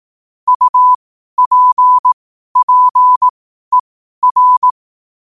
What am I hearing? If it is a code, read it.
Morse code, heard as a keyed tone, about 18 wpm, spaced as for 12 wpm, 1000 Hz. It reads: UPPER
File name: qotdfiles / 12wpm